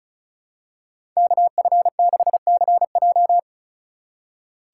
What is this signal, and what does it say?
Morse code, heard as a keyed tone, about 36 wpm, 700 Hz. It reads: KF6CJ
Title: USA Call Signs - 5 Characters - 2x Character Spacing - 35wpm